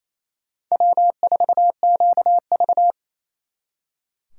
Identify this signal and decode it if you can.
Morse code, heard as a keyed tone, about 33 wpm, 700 Hz. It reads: W4QV
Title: Speed Racing - USA 1x2 Call Signs 28wpm